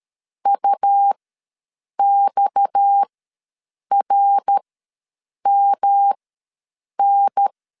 Morse code, heard as a keyed tone, about 13 wpm, spaced as for 7 wpm, 800 Hz. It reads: UXRMN
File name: LEZIONI / 7 GF9Z / 7 GF9 ws 12wpm 3344